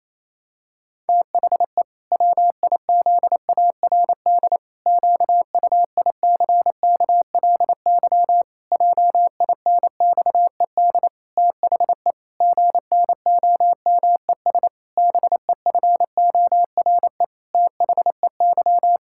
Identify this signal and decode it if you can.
Morse code, heard as a keyed tone, about 28 wpm, 700 Hz. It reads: THE WIZARD QUICKLY JINXED THE GNOMES BEFORE THEY
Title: Pangram Challenge v2 - 28wpm